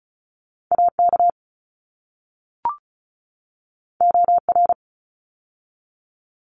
Morse code, heard as a keyed tone, about 34 wpm, 700 Hz. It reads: AK OR
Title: ARRL Sections 35wpm